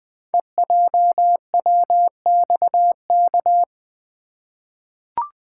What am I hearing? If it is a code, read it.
Morse code, heard as a keyed tone, about 20 wpm, 700 Hz. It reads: EJWXK
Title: Code Groups - 5 Letters - 20wpm